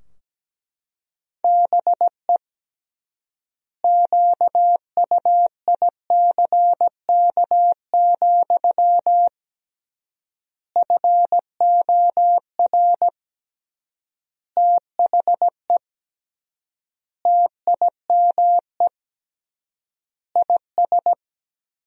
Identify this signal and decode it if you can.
Morse code, heard as a keyed tone, about 17 wpm, 700 Hz. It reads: BE QUICK, FOR THE TIME IS
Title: 3x Word-Spacing - Sentences from Top 500 Words - 17wpm